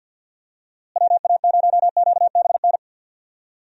Morse code, heard as a keyed tone, about 50 wpm, 700 Hz. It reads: WA0QBN
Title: USA Call Signs - 6 Characters - Repeated 3x - 50wpm